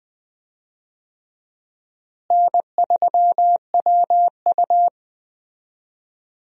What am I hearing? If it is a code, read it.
Morse code, heard as a keyed tone, about 20 wpm, 700 Hz. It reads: N3WU
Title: USA Call Signs - 4 Characters - Repeated 2x - 20wpm